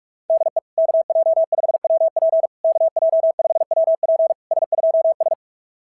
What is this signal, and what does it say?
Morse code, heard as a keyed tone, about 44 wpm, 690 Hz. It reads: DE KJ5WP KJ5WP S2S
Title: SOTA QSO Practice 45wpm